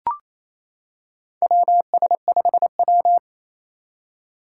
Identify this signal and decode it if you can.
Morse code, heard as a keyed tone, about 28 wpm, 700 Hz. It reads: WS5W
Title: Four Challenging Letter-Number Combinations 28wpm